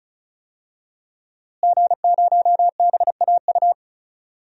Morse code, heard as a keyed tone, about 36 wpm, 700 Hz. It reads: G0BAU
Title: DXCC-100 Call Signs - 5 Characters - Repeated 3x - 35wpm